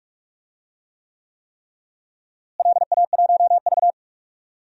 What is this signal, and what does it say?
Morse code, heard as a keyed tone, about 43 wpm, 700 Hz. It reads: LA1V
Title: DXCC-100 Call Signs - 4 Characters - Repeated 3x - 45wpm